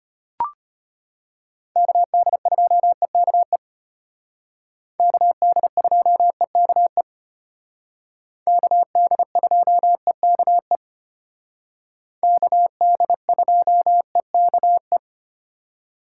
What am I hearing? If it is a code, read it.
Morse code, heard as a keyed tone, about 32 wpm, 700 Hz. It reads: KD2EKE KD2EKE KD2EKE KD2EKE